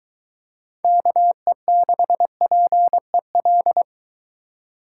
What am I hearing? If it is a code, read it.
Morse code, heard as a keyed tone, about 22 wpm, 700 Hz. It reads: KE6PEL